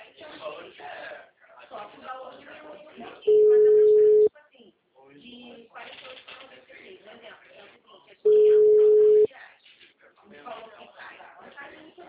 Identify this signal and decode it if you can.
Ringback tone (Europe, 425 Hz), with speech.
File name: out- -2016- - - .26537